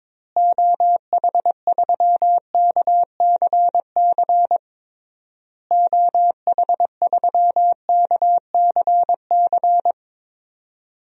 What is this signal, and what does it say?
Morse code, heard as a keyed tone, about 22 wpm, 700 Hz. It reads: OH3KCC OH3KCC
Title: DXCC-100 Call Signs - 6 Characters - Repeated 3x - 22wpm